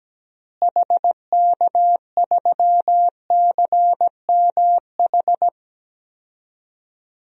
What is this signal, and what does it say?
Morse code, heard as a keyed tone, about 17 wpm, 700 Hz. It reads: HK3CMH
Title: DXCC-100 Call Signs - 6 Characters - Repeated 3x - 17wpm